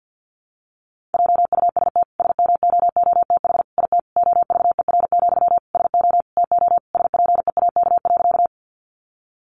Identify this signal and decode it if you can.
Morse code, heard as a keyed tone, about 50 wpm, 700 Hz. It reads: JUST SMOOTH IT OVER, SO TO SPEAK.
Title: Sentences from Top 1400 Words 50wpm